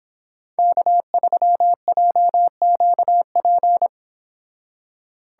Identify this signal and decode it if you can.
Morse code, heard as a keyed tone, about 25 wpm, 700 Hz. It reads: K3JQP